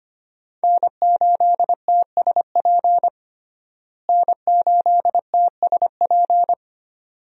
Morse code, heard as a keyed tone, about 25 wpm, 700 Hz. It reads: N8TSP N8TSP
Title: USA Call Signs - 5 Characters - Repeated 2x - 25wpm